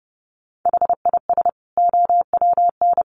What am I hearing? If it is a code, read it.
Morse code, heard as a keyed tone, about 30 wpm, 700 Hz. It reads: HIS OWN
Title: Sets of 2 Words 30wpm